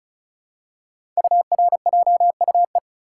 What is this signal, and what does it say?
Morse code, heard as a keyed tone, about 36 wpm, 700 Hz. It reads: URJUE
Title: Code Groups - 5 Letters - 35wpm